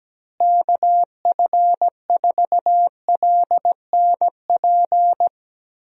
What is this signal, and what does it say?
Morse code, heard as a keyed tone, about 17 wpm, 700 Hz. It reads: KF4LNP
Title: USA Call Signs - 6 Characters - Repeated 3x - 17wpm